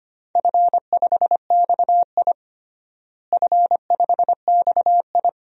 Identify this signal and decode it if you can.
Morse code, heard as a keyed tone, about 25 wpm, 700 Hz. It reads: F5XI F5XI
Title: DXCC-100 Call Signs - 4 Characters - Repeated 2x - 25wpm